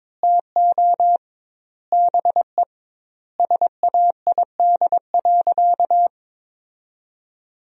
Morse code, heard as a keyed tone, about 22 wpm, 700 Hz. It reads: TO BE SAID.
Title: 2x Word-Spacing - Sentences from Top 100 Words - 22wpm